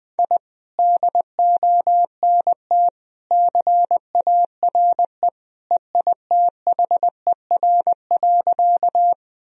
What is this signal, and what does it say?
Morse code, heard as a keyed tone, about 20 wpm, 700 Hz. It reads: I DONT CARE EITHER.
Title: Sentences from Top 1000 Words - Review Part 4 - 20wpm